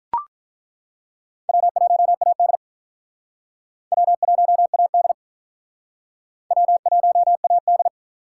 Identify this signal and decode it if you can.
Morse code, heard as a keyed tone, about 43 wpm, 700 Hz. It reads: W1AD W1AD W1AD
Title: Speed Racing - USA 1x2 Call Signs 35wpm